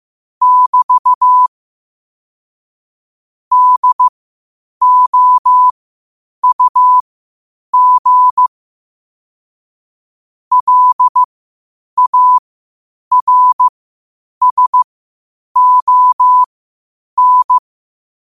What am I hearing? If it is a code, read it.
Morse code, heard as a keyed tone, about 15 wpm, spaced as for 8 wpm, 1000 Hz. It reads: <BT> DOUG LARSON
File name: qotdfiles / 8wpm